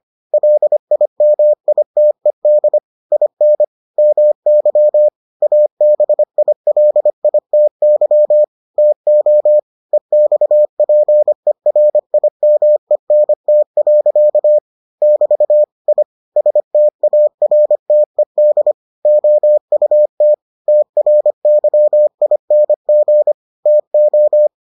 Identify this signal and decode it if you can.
Morse code, heard as a keyed tone, about 25 wpm, 600 Hz. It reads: LIMITED IN MY ABILITY TO EXPERIMENT. <BT> I STARTED OUT TRYING TO